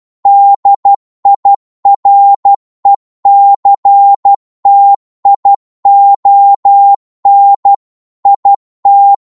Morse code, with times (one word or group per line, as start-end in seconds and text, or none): 0.3-7.7 DIRECTION
8.3-9.1 IT